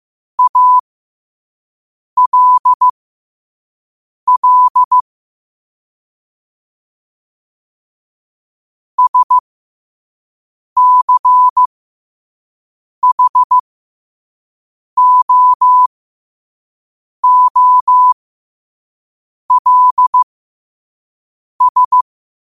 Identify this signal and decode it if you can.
Morse code, heard as a keyed tone, about 15 wpm, spaced as for 5 wpm, 1000 Hz. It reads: ALL SCHOOLS
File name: qotdfiles / 5wpm